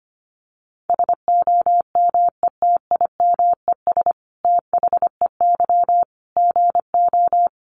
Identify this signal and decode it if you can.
Morse code, heard as a keyed tone, about 25 wpm, 700 Hz. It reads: SOMETIMES THEY GO
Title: Sentences from Top 600 Words 25wpm